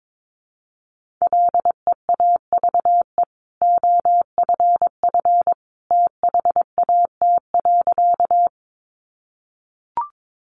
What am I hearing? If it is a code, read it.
Morse code, heard as a keyed tone, about 22 wpm, 700 Hz. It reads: LEAVE OFF THAT.
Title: Sentences from Top 400 Words 22wpm